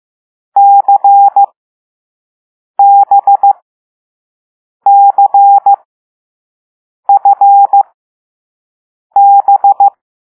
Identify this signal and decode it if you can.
Morse code, heard as a keyed tone, about 15 wpm, spaced as for 5 wpm, 800 Hz. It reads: CBCFB
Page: Play Lesson 6 C - all letters in previous lessons